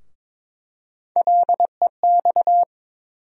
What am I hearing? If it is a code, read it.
Morse code, heard as a keyed tone, about 21 wpm, 700 Hz. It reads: LEX
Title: USA Call Sign Postfixes - 3 Letters 22wpm